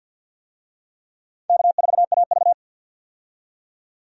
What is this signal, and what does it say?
Morse code, heard as a keyed tone, about 53 wpm, 700 Hz. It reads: K4AV
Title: USA Call Signs - 4 Characters - 50wpm